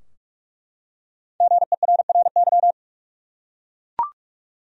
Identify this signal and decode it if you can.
Morse code, heard as a keyed tone, about 42 wpm, 700 Hz. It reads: GERRY
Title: USA Common Names - 45wpm